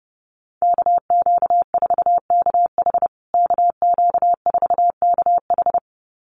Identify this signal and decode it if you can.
Morse code, heard as a keyed tone, about 30 wpm, 700 Hz. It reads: KQ4KH KQ4KH
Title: USA Call Signs Repeated 2x 30wpm